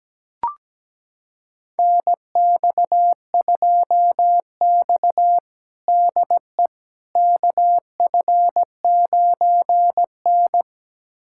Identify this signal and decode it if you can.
Morse code, heard as a keyed tone, about 17 wpm, 700 Hz. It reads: NX2X DE KF9N
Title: USA Callsign DE Callsign - 4 Characters - 17wpm